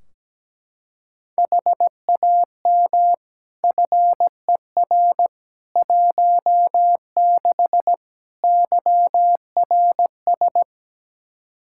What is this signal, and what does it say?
Morse code, heard as a keyed tone, about 17 wpm, 700 Hz. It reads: HAM FER 16 YRS
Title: K1USN SST QSO Practice 17wpm